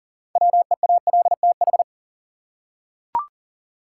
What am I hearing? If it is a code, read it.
Morse code, heard as a keyed tone, about 40 wpm, 700 Hz. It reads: WEALTH
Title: Six Letter Words - 40wpm